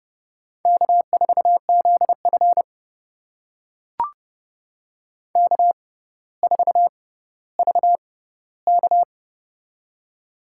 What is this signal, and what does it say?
Morse code, heard as a keyed tone, about 31 wpm, spaced as for 15 wpm, 700 Hz. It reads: K4ZF K4VK
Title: USA Call Signs - 4 Characters - 6x Character Spacing - 30wpm